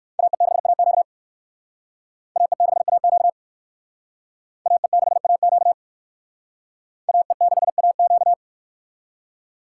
Morse code, heard as a keyed tone, about 60 wpm, 700 Hz. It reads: AE6AQ AE6AQ AE6AQ AE6AQ